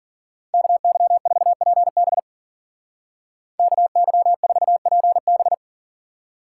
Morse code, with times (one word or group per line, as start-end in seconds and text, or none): 0.5-2.2 KY4PB
3.6-5.5 KY4PB